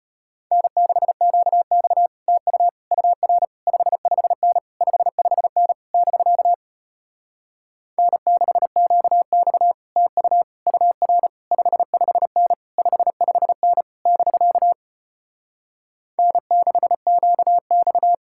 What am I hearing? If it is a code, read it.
Morse code, heard as a keyed tone, about 33 wpm, 700 Hz. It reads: N6QX TU UR 55N 55N <BK> N6QX TU UR 55N 55N <BK> N6QX